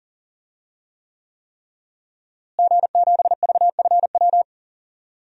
Morse code, heard as a keyed tone, about 40 wpm, 700 Hz. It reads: G7VFW
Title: DXCC-100 Call Signs - 5 Characters - Repeated 3x - 40wpm